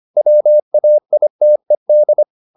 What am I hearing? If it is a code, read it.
Morse code, heard as a keyed tone, about 25 wpm, 600 Hz. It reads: WAITED
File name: the old man and the sea part 1 25wpm